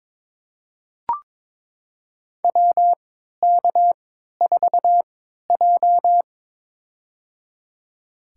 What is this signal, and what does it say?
Morse code, heard as a keyed tone, about 23 wpm, spaced as for 13 wpm, 700 Hz. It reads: WK4J
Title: USA Call Signs - 4 Characters - 3x Character Spacing - 22wpm